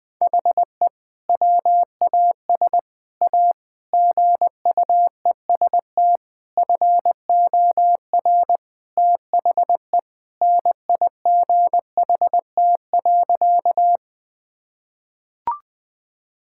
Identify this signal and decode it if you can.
Morse code, heard as a keyed tone, about 20 wpm, 700 Hz. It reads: HE WAS A GUEST FOR THE NIGHT.
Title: Sentences from Top 1600 Words 20wpm